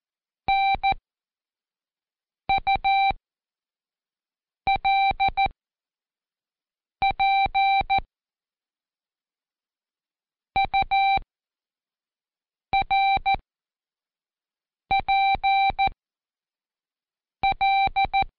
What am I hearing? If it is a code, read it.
Morse code, heard as a keyed tone, about 14 wpm, spaced as for 5 wpm, 780 Hz. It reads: NULP URPL
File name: Code Quick / AudioDownld1 / Seg37-Day4-Quiz CHLN